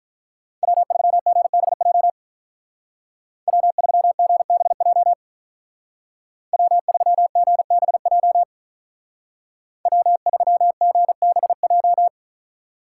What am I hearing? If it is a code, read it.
Morse code, heard as a keyed tone, about 43 wpm, 700 Hz. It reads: W3GBJ W3GBJ W3GBJ W3GBJ
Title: Speed Racing - USA 1x3 Call Signs 35wpm